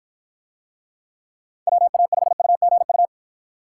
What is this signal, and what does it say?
Morse code, heard as a keyed tone, about 53 wpm, 700 Hz. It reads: WA5UGU